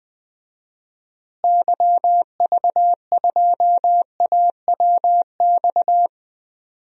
Morse code, heard as a keyed tone, about 20 wpm, 700 Hz. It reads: YV2AWX
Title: DXCC-100 Call Signs - 6 Characters - Repeated 2x - 20wpm